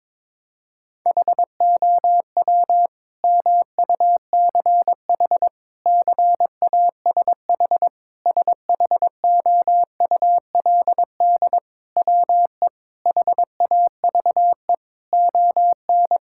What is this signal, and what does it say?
Morse code, heard as a keyed tone, about 22 wpm, 700 Hz. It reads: HOW MUCH CASH SHOULD WE HAVE ON
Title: Sentences from Top 1000 Words - Review Part 5 - 22wpm